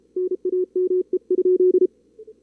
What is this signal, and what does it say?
Morse code, heard as a keyed tone, about 33 wpm, 380 Hz. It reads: NAME?
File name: CWSprint / Feb2022 / k5tr-feb-na-sprint-cw-k5ot -2300